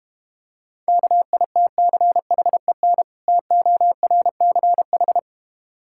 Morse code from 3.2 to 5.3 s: TORCH

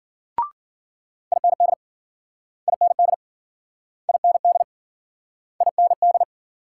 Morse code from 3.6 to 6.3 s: IND IND

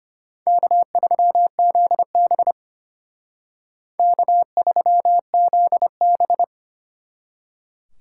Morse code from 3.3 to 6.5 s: K3ZB

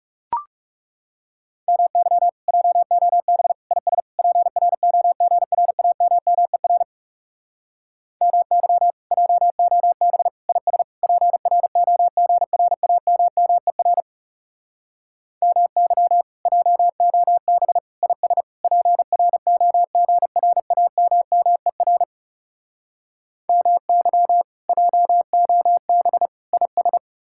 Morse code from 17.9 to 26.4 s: IS PROGRAMMER MY JOB

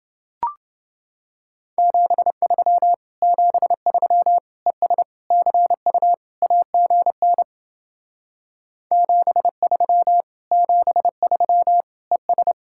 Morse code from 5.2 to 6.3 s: CU

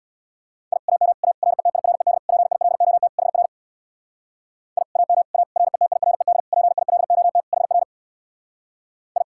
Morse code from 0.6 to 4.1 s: I AM A RETIRED PILOT HR

